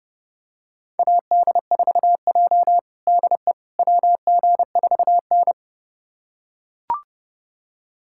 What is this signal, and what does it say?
Morse code, heard as a keyed tone, about 32 wpm, 700 Hz. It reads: AD4J DE WG4N
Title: USA Callsign DE Callsign - 4 Characters - 30wpm